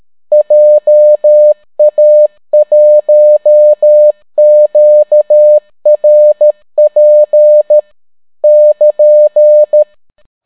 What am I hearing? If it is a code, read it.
Morse code, heard as a keyed tone, about 13 wpm, 600 Hz. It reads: JA1QRP <KN>